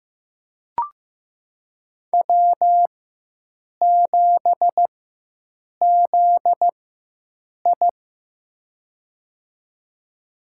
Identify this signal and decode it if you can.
Morse code, heard as a keyed tone, about 15 wpm, spaced as for 7 wpm, 700 Hz. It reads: W7ZI
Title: USA Call Signs - 4 Characters - 4x Character Spacing - 15wpm